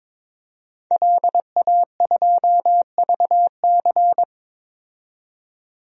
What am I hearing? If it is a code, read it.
Morse code, heard as a keyed tone, about 22 wpm, 700 Hz. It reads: LA2VC
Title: DXCC-100 Call Signs - 5 Characters - 22wpm